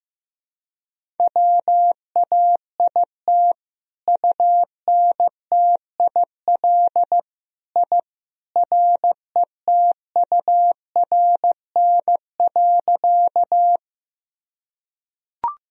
Morse code, heard as a keyed tone, about 16 wpm, 700 Hz. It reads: WAIT UNTIL I RETURN.
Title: Sentences from Top 1000 Words - Review Part 1 - 15wpm